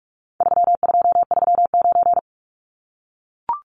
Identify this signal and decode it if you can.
Morse code, heard as a keyed tone, about 44 wpm, 700 Hz. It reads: 3239
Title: Numbers - 4-Digit 45wpm